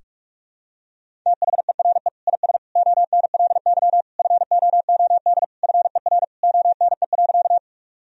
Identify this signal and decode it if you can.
Morse code, heard as a keyed tone, about 46 wpm, 700 Hz. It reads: THERE IS ONLY FOOD FER ONE.
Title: Sentences from Top 300 Words and Abbreviations 45wpm